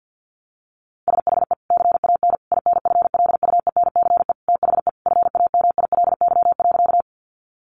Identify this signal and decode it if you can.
Morse code, heard as a keyed tone, about 49 wpm, 700 Hz. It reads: SHE CAN INFLUENCE THE FAMILY.
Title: Sentences from Top 1100 Words 50wpm